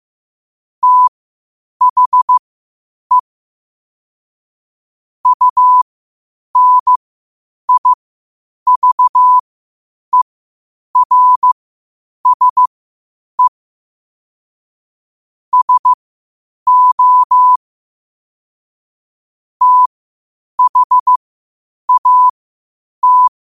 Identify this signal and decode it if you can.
Morse code, heard as a keyed tone, about 15 wpm, spaced as for 8 wpm, 1000 Hz. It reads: THE UNIVERSE SO THAT
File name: qotdfiles / 8wpm